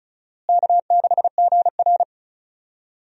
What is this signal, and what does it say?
Morse code, heard as a keyed tone, about 34 wpm, 700 Hz. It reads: K6GR
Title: USA Call Signs - 4 Characters - Repeated 3x - 35wpm